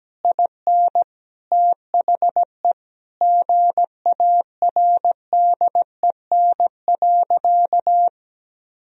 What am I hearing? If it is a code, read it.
Morse code, heard as a keyed tone, about 17 wpm, 700 Hz. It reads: IN THE GARDEN.
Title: Sentences from Top 1000 Words - Review Part 4 - 17wpm